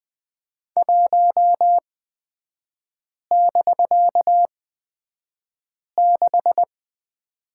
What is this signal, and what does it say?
Morse code, heard as a keyed tone, about 20 wpm, spaced as for 5 wpm, 700 Hz. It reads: 1<BK>6
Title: 250 Lesson 40e - BK - Three Characters 20wpm